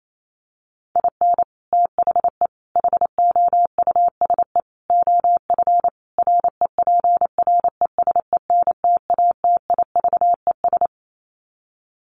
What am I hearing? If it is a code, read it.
Morse code, heard as a keyed tone, about 28 wpm, 700 Hz. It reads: IN THE HOUSE OF REPRESENTATIVES
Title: Sets of 5 Words Encore 28wpm